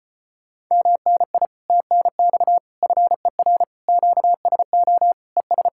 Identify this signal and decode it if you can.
Morse code, heard as a keyed tone, about 33 wpm, 700 Hz. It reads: MNI TNX FER QSO ES